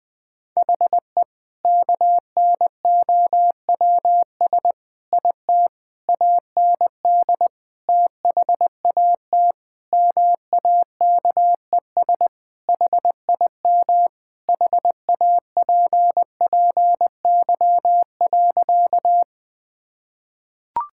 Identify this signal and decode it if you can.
Morse code, heard as a keyed tone, about 20 wpm, 700 Hz. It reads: HE KNOWS IT AND THAT MAKES HIM HAPPY.
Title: Sentences with 2-Word Phrases - 20wpm